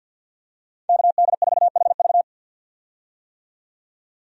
Morse code, heard as a keyed tone, about 47 wpm, 700 Hz. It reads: KD4HV
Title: USA Call Signs - 5 Characters - 50wpm